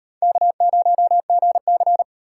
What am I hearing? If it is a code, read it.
Morse code, heard as a keyed tone, about 33 wpm, 700 Hz. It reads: K0GC